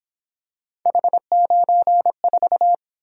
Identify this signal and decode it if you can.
Morse code, heard as a keyed tone, about 27 wpm, 700 Hz. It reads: H94
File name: Speed-Racing - Three Letter-Number Combinations - 22wpm